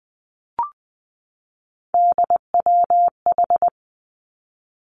Morse code, heard as a keyed tone, about 20 wpm, 700 Hz. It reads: DWH
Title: Three Challenging Letter-Number Combinations 20wpm